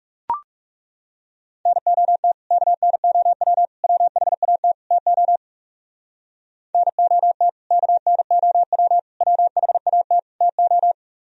Morse code, heard as a keyed tone, about 40 wpm, 700 Hz. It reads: NOT KNOW WHAT TO NOT KNOW WHAT TO